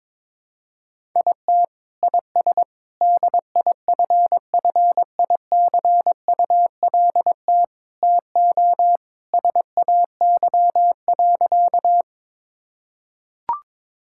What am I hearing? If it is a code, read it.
Morse code, heard as a keyed tone, about 22 wpm, 700 Hz. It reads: IT IS DIFFICULT TO SAY.
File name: practice / 22wpm / Sentences with 4-Word Phrases - 22wpm